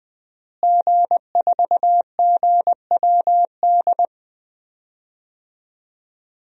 Morse code, heard as a keyed tone, about 20 wpm, 700 Hz. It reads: G4GWD
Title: DXCC-100 Call Signs - 5 Characters - Repeated 2x - 20wpm